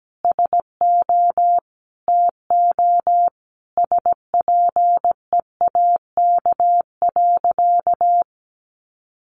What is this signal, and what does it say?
Morse code, heard as a keyed tone, about 17 wpm, 700 Hz. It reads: SO TO SPEAK.
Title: Sentences from Top 1400 Words 17wpm